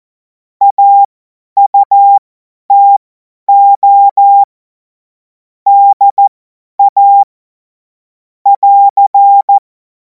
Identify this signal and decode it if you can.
Morse code, heard as a keyed tone, about 14 wpm, spaced as for 10 wpm, 800 Hz. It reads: AUTO DA <AR>
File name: Morsekurs / Witze / Witz 03. Februar 2026 50 F70